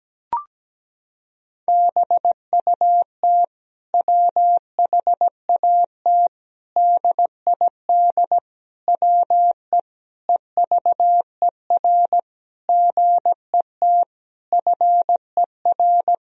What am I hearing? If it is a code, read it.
Morse code, heard as a keyed tone, about 17 wpm, 700 Hz. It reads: BUT WHAT DID WE EVER GET FER
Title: Sentences from Top 400 Words and Abbreviations 17wpm